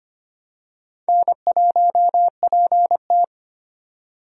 Morse code, heard as a keyed tone, about 25 wpm, 700 Hz. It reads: N1PT
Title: USA Call Signs - 4 Characters - 25wpm